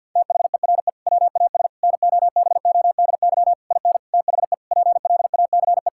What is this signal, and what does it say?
Morse code, heard as a keyed tone, about 50 wpm, 700 Hz. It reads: THERE WAS NOBODY IN THE PLACE
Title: Sentences from Top 1600 Words 50wpm